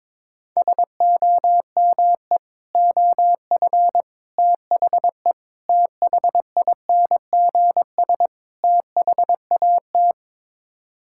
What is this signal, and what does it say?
Morse code, heard as a keyed tone, about 22 wpm, 700 Hz. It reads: SOME OF THE THINGS THAT
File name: practice / 22wpm / Build A Phrase - 5-Word Phrases - 22wpm